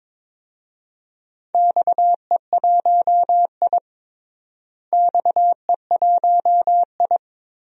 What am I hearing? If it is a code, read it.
Morse code, heard as a keyed tone, about 22 wpm, 700 Hz. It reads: XE1I XE1I
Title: DXCC-100 Call Signs - 4 Characters - Repeated 3x - 22wpm